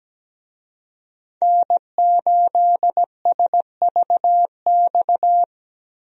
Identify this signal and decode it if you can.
Morse code, heard as a keyed tone, about 17 wpm, 700 Hz. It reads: N8SVX
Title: USA Call Signs - 5 Characters - Repeated 2x - 17wpm